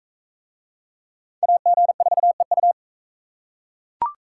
Morse code, heard as a keyed tone, about 38 wpm, 700 Hz. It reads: AG4EU